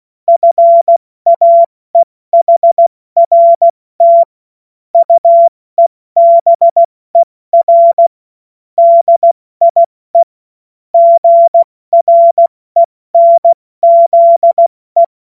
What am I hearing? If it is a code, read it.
Morse code, heard as a keyed tone, about 16 wpm, spaced as for 14 wpm, 680 Hz. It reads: FAEHRT UEBER DIE GRENZE
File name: Morsekurs / Witze / Witz 28. Dezember 2021, 70 F80